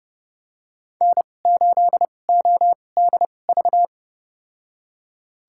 Morse code, heard as a keyed tone, about 30 wpm, spaced as for 22 wpm, 700 Hz. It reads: N8ODV
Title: USA Call Signs - 2x Character Spacing - 30wpm